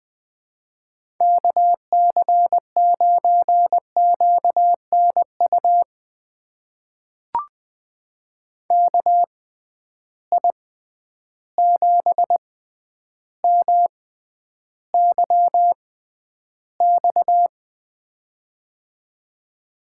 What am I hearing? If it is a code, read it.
Morse code, heard as a keyed tone, about 20 wpm, spaced as for 10 wpm, 700 Hz. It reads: KC9QNU KI7MYX
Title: USA Call Signs - 6 Characters - 6x Character Spacing - 20wpm